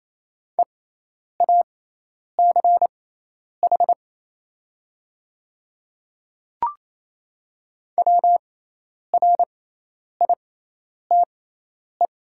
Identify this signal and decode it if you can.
Morse code, heard as a keyed tone, about 28 wpm, spaced as for 10 wpm, 700 Hz. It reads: EACH WRITE
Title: 6x Character Spacing - Top 100 Words - 28wpm